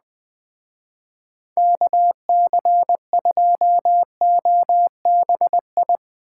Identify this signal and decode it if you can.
Morse code, heard as a keyed tone, about 20 wpm, 700 Hz. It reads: KC2OBI
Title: Speed Racing - USA 2x3 Call Signs 17wpm